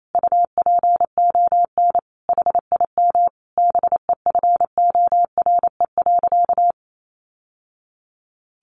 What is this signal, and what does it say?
Morse code, heard as a keyed tone, about 28 wpm, 700 Hz. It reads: UPON HIM BEFORE.
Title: Sentences from Top 1100 Words 28wpm